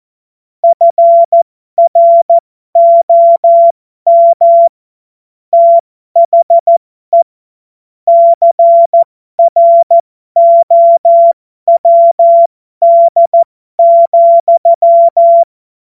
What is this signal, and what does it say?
Morse code, heard as a keyed tone, about 14 wpm, spaced as for 12 wpm, 680 Hz. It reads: FROM THE CROWD,
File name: Morsekurs / Witze